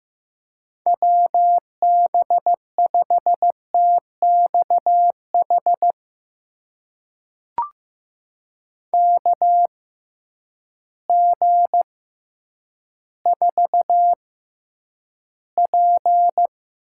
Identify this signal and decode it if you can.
Morse code, heard as a keyed tone, about 15 wpm, spaced as for 9 wpm, 700 Hz. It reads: WB5TXH KG4P
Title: USA Call Signs - 6 Characters - 6x Character Spacing - 15wpm